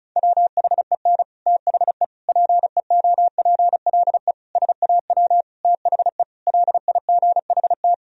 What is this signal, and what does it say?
Morse code, heard as a keyed tone, about 36 wpm, 700 Hz. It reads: WHEN THE PEOPLE SAW THE LIGHT
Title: Sentences from Top 1000 Words - Review Part 1 - 35wpm